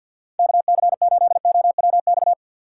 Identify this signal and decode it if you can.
Morse code, heard as a keyed tone, about 66 wpm, spaced as for 57 wpm, 700 Hz. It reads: KC8OWX
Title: Speed Racing - USA 2x3 Call Signs 50wpm